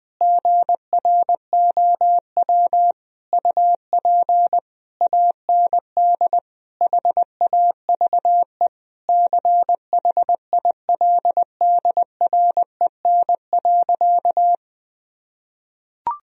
Morse code, heard as a keyed tone, about 20 wpm, 700 Hz. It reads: GROW UP AND HAVE CHILDREN.
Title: Sentences from Top 1000 Words - Review Part 1 - 20wpm